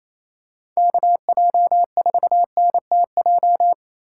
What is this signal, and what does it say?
Morse code, heard as a keyed tone, about 28 wpm, 700 Hz. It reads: KJ4NTJ
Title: USA Call Signs - 6 Characters - 4x Character Spacing - 28wpm